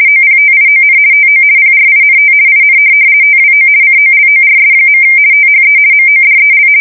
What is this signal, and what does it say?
RTTY, about 50 baud. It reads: OPTION TO PRINT IN TWO COLORS. THE MODEL
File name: fsk170-50B / teletype-corp